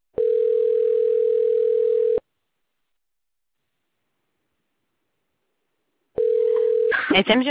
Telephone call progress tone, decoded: ringback tone